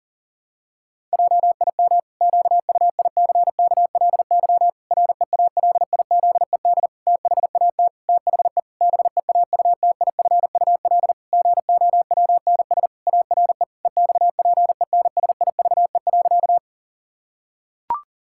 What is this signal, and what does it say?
Morse code, heard as a keyed tone, about 40 wpm, 700 Hz. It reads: JIM QUICKLY REALIZED THAT THE BEAUTIFUL GOWNS ARE EXPENSIVE.
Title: Pangram Challenge v2 - 40wpm